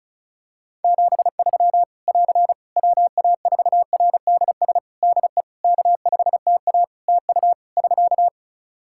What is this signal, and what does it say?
Morse code, heard as a keyed tone, about 34 wpm, 700 Hz. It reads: 73 <AR> WA4RDS DE K5TA TU <SK>
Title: Practice QSO - Exchange 4 - Ending - 35wpm